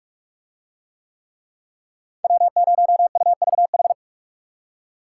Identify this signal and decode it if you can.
Morse code, heard as a keyed tone, about 40 wpm, 700 Hz. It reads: W0UVH